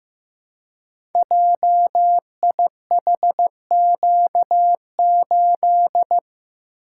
Morse code, heard as a keyed tone, about 15 wpm, 700 Hz. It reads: JIHQ8
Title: Code Groups - 5 Letter:Numbers - 15wpm